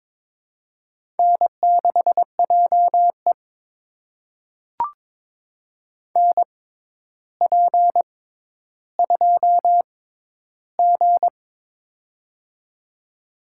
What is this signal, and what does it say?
Morse code, heard as a keyed tone, about 22 wpm, spaced as for 11 wpm, 700 Hz. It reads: N6JE NP2G
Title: USA Call Signs - 4 Characters - 6x Character Spacing - 22wpm